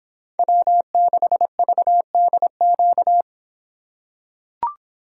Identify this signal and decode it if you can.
Morse code, heard as a keyed tone, about 26 wpm, 700 Hz. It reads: W6VDQ